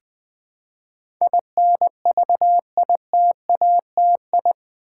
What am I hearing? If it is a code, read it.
Morse code, heard as a keyed tone, about 20 wpm, 700 Hz. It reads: INVITATI
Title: Build A Word - 4 Letters At A Time - 10-Letter Words - 20wpm